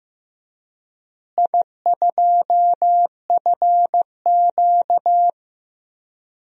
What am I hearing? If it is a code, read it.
Morse code, heard as a keyed tone, about 16 wpm, 700 Hz. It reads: I2FQ
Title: DXCC-100 Call Signs - 4 Characters - Repeated 3x - 15wpm